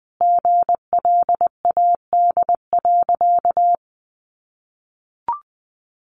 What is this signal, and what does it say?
Morse code, heard as a keyed tone, about 20 wpm, 700 Hz. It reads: GLAD.
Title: Sentences from Top 800 Words 20wpm